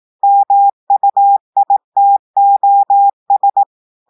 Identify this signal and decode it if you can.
Morse code, heard as a keyed tone, about 18 wpm, 800 Hz. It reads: MUITOS
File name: audios / portugais / portugais texte en clair 2 vitesse 18 mots minutes